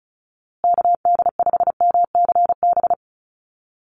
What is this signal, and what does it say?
Morse code, heard as a keyed tone, about 34 wpm, 700 Hz. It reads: KD5MCB
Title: USA Call Signs Repeated 3x 35wpm